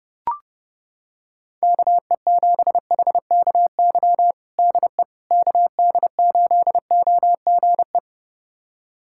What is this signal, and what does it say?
Morse code, heard as a keyed tone, about 30 wpm, 700 Hz. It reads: KE7HKY DE KD8OGE
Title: USA Callsign DE Callsign - 6 Characters - 30wpm